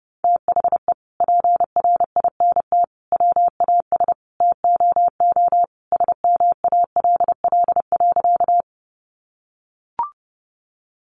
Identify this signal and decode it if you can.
Morse code, heard as a keyed tone, about 29 wpm, 700 Hz. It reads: THE PRINT WAS TOO SMALL.
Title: Sentences from Top 1400 Words 30wpm